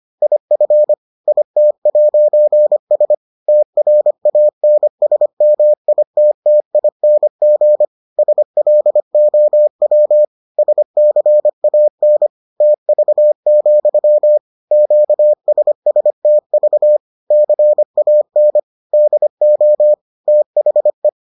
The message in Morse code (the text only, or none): IF IT'S TRANSMITTING SLOW SCAN TV, QSSTV CAN DO THE